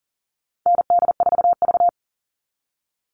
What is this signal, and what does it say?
Morse code, heard as a keyed tone, about 40 wpm, 700 Hz. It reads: ND4V
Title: Call Signs Repeated 3x - 40wpm